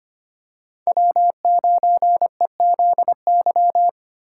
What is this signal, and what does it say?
Morse code, heard as a keyed tone, about 25 wpm, 700 Hz. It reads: W9EZY
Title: USA Call Signs - 5 Characters - 3x Character Spacing - 25wpm